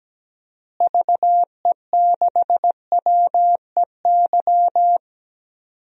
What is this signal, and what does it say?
Morse code, heard as a keyed tone, about 17 wpm, 700 Hz. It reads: VE6WEY
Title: DXCC-100 Call Signs - 6 Characters - Repeated 2x - 17wpm